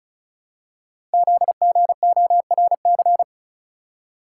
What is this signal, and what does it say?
Morse code, heard as a keyed tone, about 36 wpm, 700 Hz. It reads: ZGORC
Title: Code Groups - 5 Letters - 35wpm